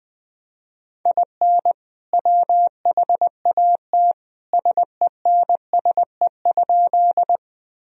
Morse code, heard as a keyed tone, about 21 wpm, 700 Hz. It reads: IN WHAT SENSE?
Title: Sentences from Top 1000 Words - Review Part 4 - 20wpm